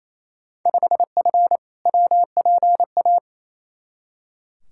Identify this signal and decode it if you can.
Morse code, heard as a keyed tone, about 28 wpm, 700 Hz. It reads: 5F WPA
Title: Field Day Exchanges Encore - 28wpm